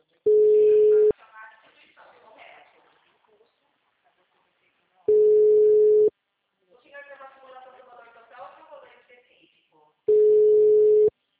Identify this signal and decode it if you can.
Ringback tone (Europe, 425 Hz), with speech.
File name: out- -2008- - - .6932